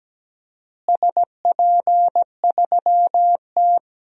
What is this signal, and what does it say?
Morse code, heard as a keyed tone, about 17 wpm, 700 Hz. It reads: SP3T
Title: DXCC-100 Call Signs - 4 Characters - Repeated 3x - 17wpm